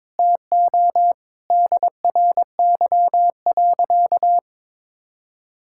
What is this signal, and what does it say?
Morse code, heard as a keyed tone, about 21 wpm, 700 Hz. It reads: TO DRY.
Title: Sentences from Top 1600 Words 22wpm